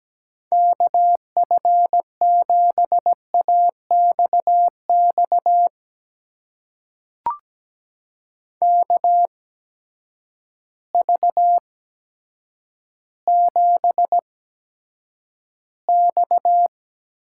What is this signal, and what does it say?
Morse code, heard as a keyed tone, about 17 wpm, spaced as for 9 wpm, 700 Hz. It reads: KF7AXX KV7X
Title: USA Call Signs - 8x Character Spacing - 17wpm